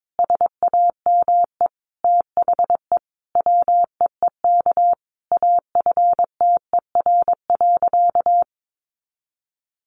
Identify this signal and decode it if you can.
Morse code, heard as a keyed tone, about 22 wpm, 700 Hz. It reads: SAME THE WEEK AFTER.
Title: Sentences from Top 500 Words 22wpm